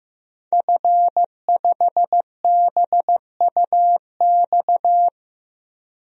Morse code, heard as a keyed tone, about 15 wpm, 700 Hz. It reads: F5BUX
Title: DXCC-100 Call Signs - 5 Characters - 15wpm